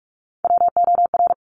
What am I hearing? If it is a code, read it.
Morse code, heard as a keyed tone, about 46 wpm, 700 Hz. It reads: WOR
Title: Top Three Letter Combinations in Words 45wpm